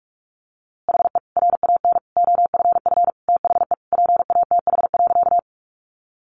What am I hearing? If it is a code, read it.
Morse code, heard as a keyed tone, about 44 wpm, 700 Hz. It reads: HE RAN OFF THE PATH.
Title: A Sound Of Thunder 45wpm